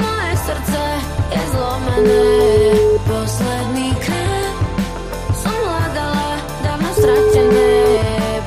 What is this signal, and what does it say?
Ringback tone (Europe, 425 Hz), with noise.